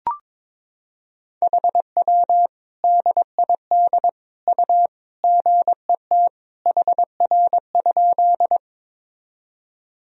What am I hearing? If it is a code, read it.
Morse code, heard as a keyed tone, about 23 wpm, 700 Hz. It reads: HW DID U GET HR?
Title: Sentences from Top 200 Words and Abbreviations 22wpm